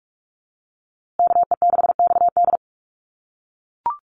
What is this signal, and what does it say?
Morse code, heard as a keyed tone, about 44 wpm, 700 Hz. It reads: KE6XD
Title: USA Call Signs Repeated 3x 45wpm